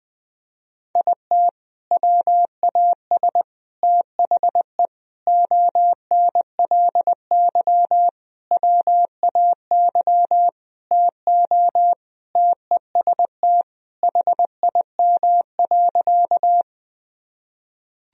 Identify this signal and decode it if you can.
Morse code, heard as a keyed tone, about 21 wpm, 700 Hz. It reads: IT WAS THE ONLY WAY TO TEST HIM.
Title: Sentences from Top 1000 Words - Review Part 3 - 20wpm